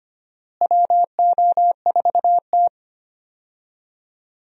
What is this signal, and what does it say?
Morse code, heard as a keyed tone, about 25 wpm, 700 Hz. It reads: WO4T
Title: USA Call Signs - 4 Characters - Repeated 3x - 25wpm